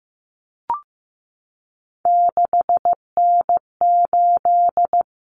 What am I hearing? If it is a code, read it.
Morse code, heard as a keyed tone, about 15 wpm, 700 Hz. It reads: 6N8
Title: Common Cut Numbers - 3-Digit 15wpm